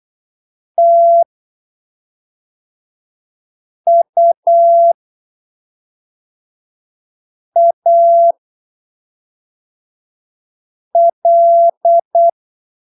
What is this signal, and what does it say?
Morse code, heard as a keyed tone, about 8 wpm, spaced as for 3 wpm, 680 Hz. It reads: TUAL
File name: arquivos / texto / texto 05 08ppm